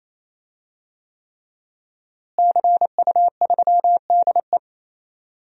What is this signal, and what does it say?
Morse code, heard as a keyed tone, about 28 wpm, 700 Hz. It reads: CU3DE
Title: DXCC-100 Call Signs - 5 Characters - Repeated 3x - 28wpm